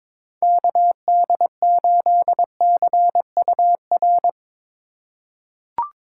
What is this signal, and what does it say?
Morse code, heard as a keyed tone, about 22 wpm, 700 Hz. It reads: KD8CUR
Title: USA Call Signs - 6 Characters - 4x Character Spacing - 22wpm